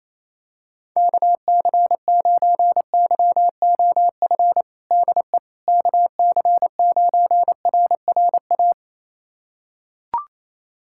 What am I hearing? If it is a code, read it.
Morse code, heard as a keyed tone, about 28 wpm, 700 Hz. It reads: KC9YOF DE KC9RRA
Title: USA Callsign DE Callsign 28wpm